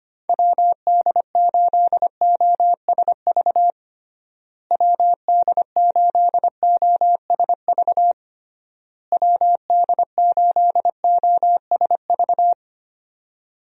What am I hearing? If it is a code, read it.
Morse code, heard as a keyed tone, about 25 wpm, 700 Hz. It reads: WD8OSV WD8OSV WD8OSV
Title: USA Call Signs - 6 Characters - Repeated 3x - 25wpm